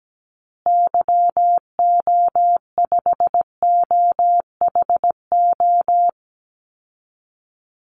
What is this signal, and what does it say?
Morse code, heard as a keyed tone, about 17 wpm, 700 Hz. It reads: YO5OHO
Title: Call Signs Repeated 3x - 17wpm